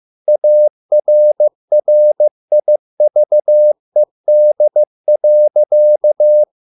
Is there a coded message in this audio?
Yes: Morse code from 0.3 to 6.4 s, heard as a keyed tone, about 15 wpm, 600 Hz. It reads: ARRIVED.